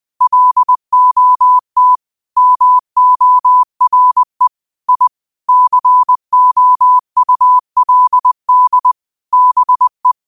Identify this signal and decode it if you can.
Morse code, heard as a keyed tone, about 20 wpm, 1000 Hz. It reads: LOT MORE I COULD BE
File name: qotdfiles / 20wpm